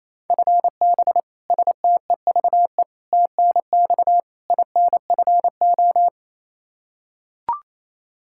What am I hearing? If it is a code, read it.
Morse code, heard as a keyed tone, about 28 wpm, 700 Hz. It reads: FB STEVE TNX INFO
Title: Practice QSO - Exchange 3 - More Info - 28wpm